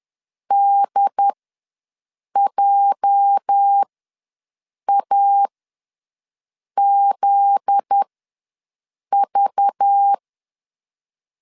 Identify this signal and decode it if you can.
Morse code, heard as a keyed tone, about 10 wpm, spaced as for 6 wpm, 800 Hz. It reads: DJAZV
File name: LEZIONI / 8 YJ45 / 8 YJ ws 10wpm 4043